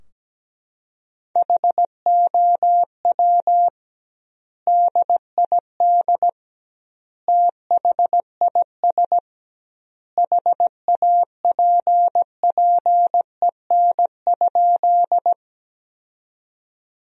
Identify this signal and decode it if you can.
Morse code, heard as a keyed tone, about 17 wpm, 700 Hz. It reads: HOW DID THIS HAPPEN?
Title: 2x Word-Spacing - Sentences from Top 400 Words - 17wpm